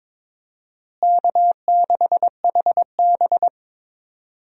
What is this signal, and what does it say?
Morse code, heard as a keyed tone, about 22 wpm, 700 Hz. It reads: K6HB
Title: USA Call Signs - 4 Characters - Repeated 2x - 22wpm